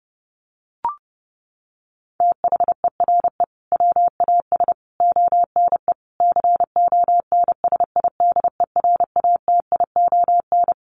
Morse code, heard as a keyed tone, about 30 wpm, 700 Hz. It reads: THERE WAS ONE CONSIDERATION
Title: Sentences from Top 1500 Words 30wpm